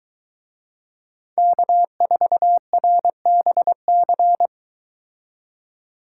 Morse code, heard as a keyed tone, about 23 wpm, 700 Hz. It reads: K4RBC